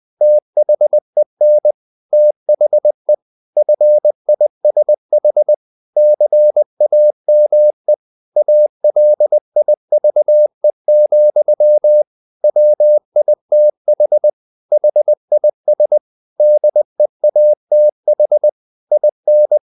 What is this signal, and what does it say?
Morse code, heard as a keyed tone, about 20 wpm, 600 Hz. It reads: THEN THE FISH CAME ALIVE, WITH HIS DEATH IN
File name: the old man and the sea part 2 20wpm